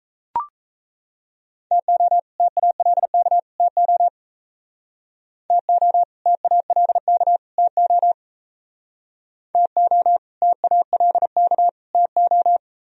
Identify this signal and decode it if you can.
Morse code, heard as a keyed tone, about 36 wpm, 700 Hz. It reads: TO TALK TO TO TALK TO TO TALK TO